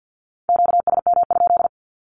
Morse code, heard as a keyed tone, about 48 wpm, 700 Hz. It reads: QSM?
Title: Q-Codes 50wpm